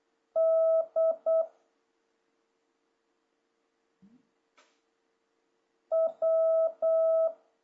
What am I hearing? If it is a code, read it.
Morse code, heard as a keyed tone, about 8 wpm, 650 Hz. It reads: D W
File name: jh8cbh / 2026cw